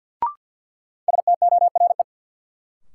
Morse code, heard as a keyed tone, about 48 wpm, 700 Hz. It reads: STORE
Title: Sentences from Top 1000 Words - Review Part 5 - New Word Drill - 50wpm